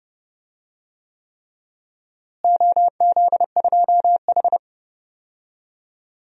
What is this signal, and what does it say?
Morse code, heard as a keyed tone, about 32 wpm, 700 Hz. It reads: OZ2H
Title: DXCC-100 Call Signs - 4 Characters - Repeated 3x - 30wpm